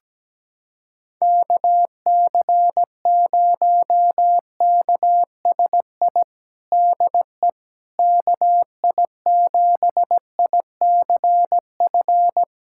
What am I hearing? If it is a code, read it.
Morse code, heard as a keyed tone, about 17 wpm, 700 Hz. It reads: KC0KSI DE KI7ICF
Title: USA Callsign DE Callsign - 6 Characters - 17wpm